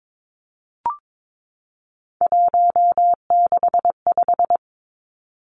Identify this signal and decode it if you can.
Morse code, heard as a keyed tone, about 22 wpm, 700 Hz. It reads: 165
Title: Three Challenging Letter-Number Combinations 22wpm